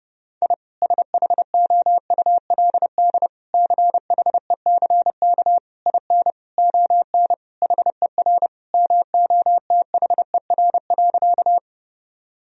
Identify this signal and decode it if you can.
Morse code, heard as a keyed tone, about 30 wpm, 700 Hz. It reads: I SHOULD CHECK IN ON HER MOTHER.
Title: Sentences from Top 1000 Words - Review Part 3 - 30wpm